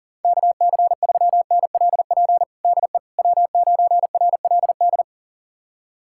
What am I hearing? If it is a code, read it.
Morse code, heard as a keyed tone, about 37 wpm, 700 Hz. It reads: KC3NLP DE W9RLD
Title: USA Callsign DE Callsign 40wpm